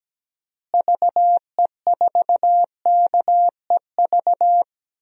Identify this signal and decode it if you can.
Morse code, heard as a keyed tone, about 17 wpm, 700 Hz. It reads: VE4KEV
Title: DXCC-100 Call Signs - 6 Characters - Repeated 3x - 17wpm